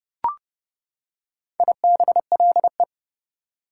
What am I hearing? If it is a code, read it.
Morse code, heard as a keyed tone, about 31 wpm, 700 Hz. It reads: IBLE
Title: Prefixes and Suffixes 30wpm